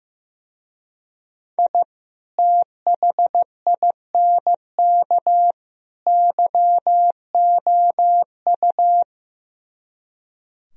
Morse code, heard as a keyed tone, about 15 wpm, 700 Hz. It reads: I THINK YOU
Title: Speed-Racing - Sets of 3 Words 15wpm